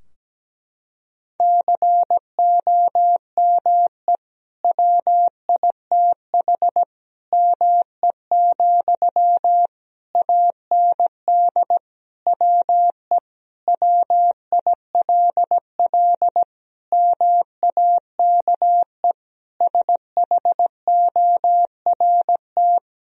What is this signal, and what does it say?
Morse code, heard as a keyed tone, about 17 wpm, 700 Hz. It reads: COME WITH ME, AND WE WILL MAKE SHORT
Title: Sentences from Top 1000 Words - Review Part 2 - 17wpm